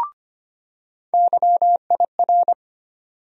Morse code, heard as a keyed tone, about 25 wpm, 700 Hz. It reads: YIR
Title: USA Call Sign Postfixes - 3 Letters 25wpm